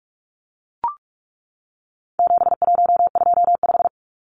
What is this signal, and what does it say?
Morse code, heard as a keyed tone, about 44 wpm, 700 Hz. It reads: 7125
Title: Numbers - 4-Digit 45wpm